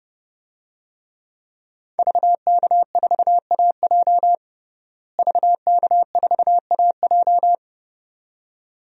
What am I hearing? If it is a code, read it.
Morse code, heard as a keyed tone, about 31 wpm, 700 Hz. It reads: VK4AJ VK4AJ
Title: DXCC-100 Call Signs - 5 Characters - Repeated 2x - 30wpm